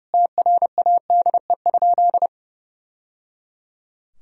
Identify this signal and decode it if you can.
Morse code, heard as a keyed tone, about 30 wpm, 700 Hz. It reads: TRADE?
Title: Sentences from Top 1000 Words - Review Part 4 - 30wpm